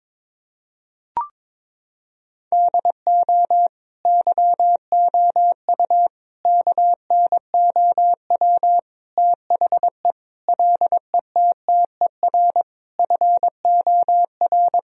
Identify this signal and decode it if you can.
Morse code, heard as a keyed tone, about 22 wpm, 700 Hz. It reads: DO YOU KNOW THE LETTER FOR
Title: Sentences from Top 1000 Words - Review Part 2 - 22wpm